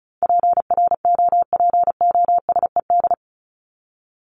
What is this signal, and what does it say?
Morse code, heard as a keyed tone, about 34 wpm, 700 Hz. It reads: PROPOSED
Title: ED Suffix - Top 100 Words with Base Words 35wpm